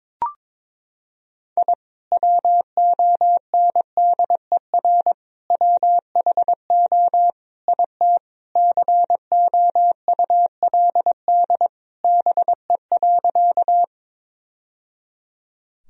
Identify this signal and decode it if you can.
Morse code, heard as a keyed tone, about 21 wpm, 700 Hz. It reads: I WONDER WHO IT COULD BE.
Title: Sentences from Top 1000 Words - Review Part 3 - 22wpm